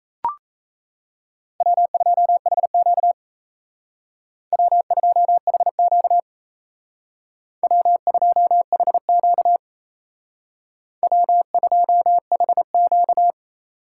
Morse code, heard as a keyed tone, about 32 wpm, 700 Hz. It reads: W2HQ W2HQ W2HQ W2HQ
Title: Speed Racing - USA 1x2 Call Signs 28wpm